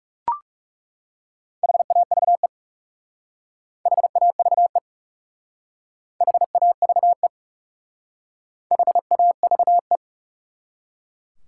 Morse code, heard as a keyed tone, about 36 wpm, 700 Hz. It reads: HAVE HAVE HAVE HAVE
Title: Speed-Racing Top 180 QSO Elements 30wpm